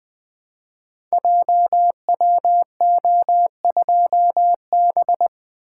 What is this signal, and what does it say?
Morse code, heard as a keyed tone, about 20 wpm, 700 Hz. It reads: JWO2B
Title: Code Groups - 5 Letter:Numbers - 20wpm